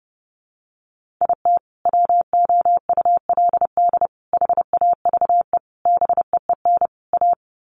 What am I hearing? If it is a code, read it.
Morse code, heard as a keyed tone, about 30 wpm, 700 Hz. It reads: IT WOULD HAVE BEEN A
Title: Sets of 5 Words Encore 30wpm